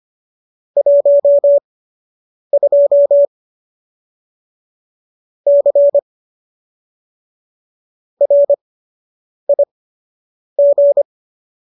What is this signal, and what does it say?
Morse code, heard as a keyed tone, about 25 wpm, spaced as for 8 wpm, 570 Hz. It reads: 12 C RIG
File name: qso003 08